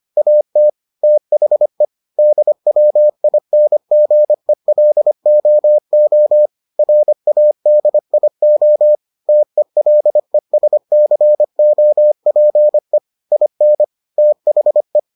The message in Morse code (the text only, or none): AT THE DWINGELOO RADIO TELESCOPE IN THE